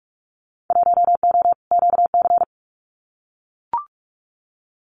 Morse code, heard as a keyed tone, about 44 wpm, 700 Hz. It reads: 1O QC
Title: Winter Field Day Exchanges 45wpm